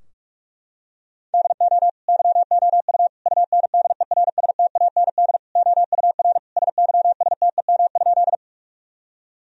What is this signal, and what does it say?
Morse code, heard as a keyed tone, about 46 wpm, 700 Hz. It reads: DO YOU UNDERSTAND OUR SYSTEM?
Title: Sentences from Top 1000 Words - Review Part 3 - 45wpm